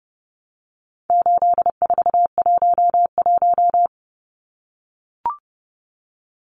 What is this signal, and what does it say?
Morse code, heard as a keyed tone, about 30 wpm, 700 Hz. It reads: 8411
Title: Numbers - 4-Digit 30wpm